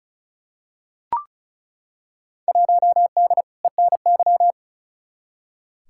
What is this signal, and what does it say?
Morse code, heard as a keyed tone, about 36 wpm, 700 Hz. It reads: 1D ENY
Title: Field Day Exchanges - 35wpm